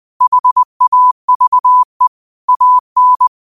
Morse code, heard as a keyed tone, about 20 wpm, 1000 Hz. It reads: HAVE AN
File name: qotdfiles / 20wpm